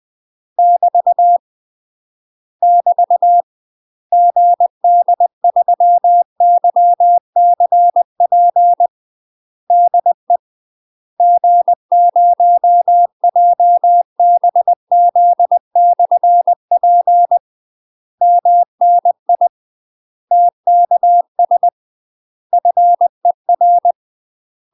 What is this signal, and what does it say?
Morse code, heard as a keyed tone, about 20 wpm, 700 Hz. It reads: <BT> <BT> GD3YCP DE G0JBZ/P MNI TKS FER
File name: UK MASTER QSOs TRAINING / 20WPM / UK MASTER QSOs 0038